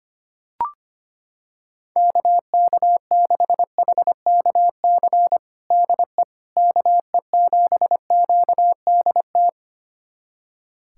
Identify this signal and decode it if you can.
Morse code, heard as a keyed tone, about 25 wpm, 700 Hz. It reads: KK6HKC DE KE7QDT
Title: USA Callsign DE Callsign - 6 Characters - 25wpm